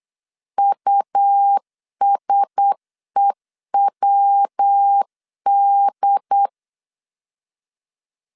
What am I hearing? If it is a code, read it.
Morse code, heard as a keyed tone, about 8 wpm, 800 Hz. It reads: USEWD
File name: LEZIONI / 6 PKX0 / 6 P 8wpm 3025